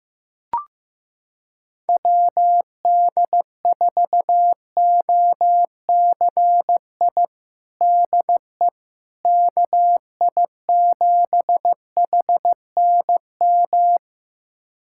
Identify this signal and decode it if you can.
Morse code, heard as a keyed tone, about 15 wpm, 700 Hz. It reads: WD4OCI DE KI7HNM
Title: USA Callsign DE Callsign - 6 Characters - 15wpm